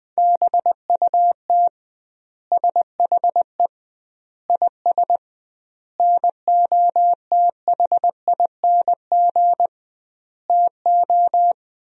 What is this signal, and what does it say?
Morse code, heard as a keyed tone, about 21 wpm, 700 Hz. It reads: BUT SHE IS NOTHING TO
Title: 2x Word-Spacing - Sentences from Top 500 Words - 20wpm